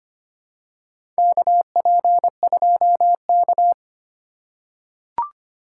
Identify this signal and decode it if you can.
Morse code, heard as a keyed tone, about 25 wpm, 700 Hz. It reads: KP2K
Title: USA Call Signs - 4 Characters - 4x Character Spacing - 25wpm